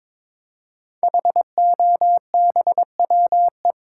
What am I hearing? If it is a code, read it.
Morse code, heard as a keyed tone, about 22 wpm, 700 Hz. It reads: HOBWE
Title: Code Groups - 5 Letters - 22wpm